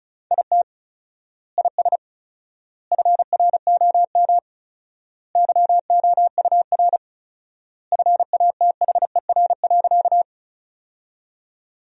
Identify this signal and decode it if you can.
Morse code, heard as a keyed tone, about 36 wpm, 700 Hz. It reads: IT IS FROM YOUR FATHER.
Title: 4x Word-Spacing - Sentences from Top 300 Words - 35wpm